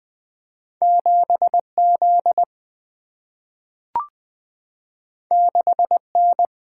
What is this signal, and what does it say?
Morse code, heard as a keyed tone, about 20 wpm, 690 Hz. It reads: 7Z 6N
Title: Two Letter-Number Combinations - 20wpm